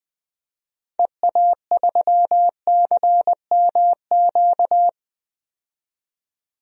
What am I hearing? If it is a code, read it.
Morse code, heard as a keyed tone, about 21 wpm, 700 Hz. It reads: EA3CMQ
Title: DXCC-100 Call Signs - 6 Characters - 20wpm